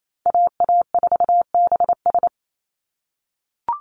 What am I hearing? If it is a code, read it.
Morse code, heard as a keyed tone, about 28 wpm, 700 Hz. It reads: AA4BS
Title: USA Call Signs 28wpm